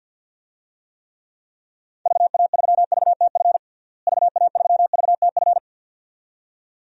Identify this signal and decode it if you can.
Morse code, heard as a keyed tone, about 53 wpm, 700 Hz. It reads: VA3VTF VA3VTF
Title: DXCC-100 Call Signs - 6 Characters - Repeated 2x - 50wpm